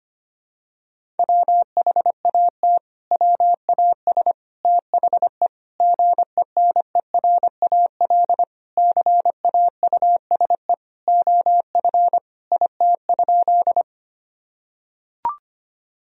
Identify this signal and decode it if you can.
Morse code, heard as a keyed tone, about 25 wpm, 700 Hz. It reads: WHAT WAS THE GENERAL CAUSE OF IT?
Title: Sentences from Top 1000 Words - Review Part 3 - 25wpm